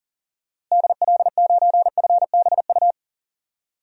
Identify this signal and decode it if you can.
Morse code, heard as a keyed tone, about 40 wpm, 700 Hz. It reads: DL9FBU
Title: DXCC-100 Call Signs - 6 Characters - Repeated 3x - 40wpm